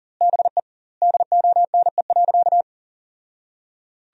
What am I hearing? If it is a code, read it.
Morse code, heard as a keyed tone, about 37 wpm, 700 Hz. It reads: BE DONE.
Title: 2x Word-Spacing - Sentences from Top 500 Words - 40wpm